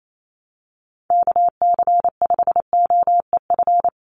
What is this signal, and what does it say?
Morse code, heard as a keyed tone, about 28 wpm, 700 Hz. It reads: KC5OEF
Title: USA Call Signs 28wpm